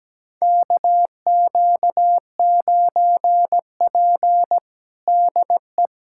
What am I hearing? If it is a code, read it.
Morse code, heard as a keyed tone, about 17 wpm, 700 Hz. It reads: KQ9P DE
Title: USA Callsign DE Callsign - 4 Characters - 17wpm